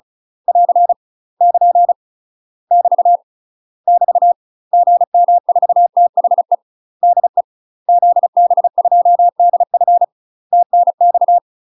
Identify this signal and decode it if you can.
Morse code, heard as a keyed tone, about 34 wpm, 700 Hz. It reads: <AR> <KN> <BT> <BT> GM4THE DE ZB2DF TNX
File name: UK MASTER QSOs TRAINING / 35WPM / UK MASTER QSOs 0000